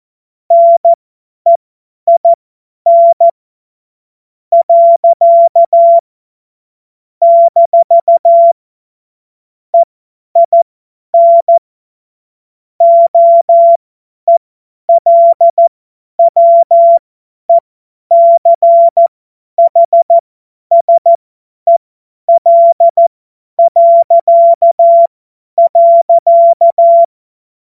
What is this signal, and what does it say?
Morse code, heard as a keyed tone, about 14 wpm, spaced as for 10 wpm, 680 Hz. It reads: NEIN . - EIN OELWECHSEL..
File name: Morsekurs / Witze / Witz 10.Oktober 2023 50 F70